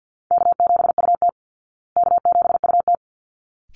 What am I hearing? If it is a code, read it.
Morse code, heard as a keyed tone, about 50 wpm, 700 Hz. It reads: K7UT K7UT
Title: Easy Call Signs Repeated 2x - 50wpm